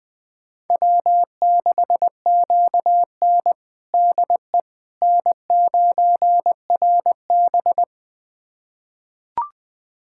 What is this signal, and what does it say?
Morse code, heard as a keyed tone, about 21 wpm, 700 Hz. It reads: W6QN DE N9RB
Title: USA Callsign DE Callsign - 4 Characters - 20wpm